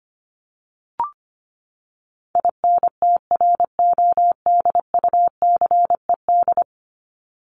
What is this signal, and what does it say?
Morse code, heard as a keyed tone, about 25 wpm, 700 Hz. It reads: INTRODUCED
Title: ED Suffix - Top 100 Words with Base Words 25wpm